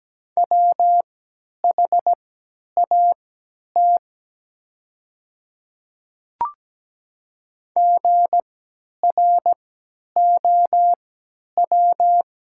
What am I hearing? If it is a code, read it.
Morse code, heard as a keyed tone, about 17 wpm, spaced as for 10 wpm, 700 Hz. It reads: WHAT GROW
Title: 3x Character Spacing - Top 300 Words - 17wpm